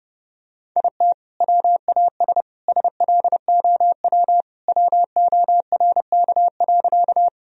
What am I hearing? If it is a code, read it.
Morse code, heard as a keyed tone, about 30 wpm, 700 Hz. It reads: IT WAS SLOW WORK.
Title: Sentences from Top 1000 Words - Review Part 2 - 30wpm